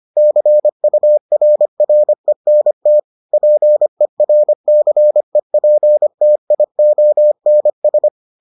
Morse code, heard as a keyed tone, about 25 wpm, 600 Hz. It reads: CURRENT PERCEPTIONS